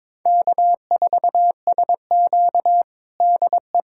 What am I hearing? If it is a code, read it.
Morse code, heard as a keyed tone, about 22 wpm, 700 Hz. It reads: K4SQ DE
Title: USA Callsign DE Callsign - 4 Characters - 22wpm